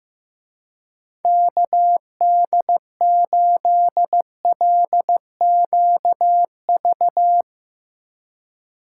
Morse code, heard as a keyed tone, about 16 wpm, 700 Hz. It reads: KD8LQV
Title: Speed Racing - USA 2x3 Call Signs 15wpm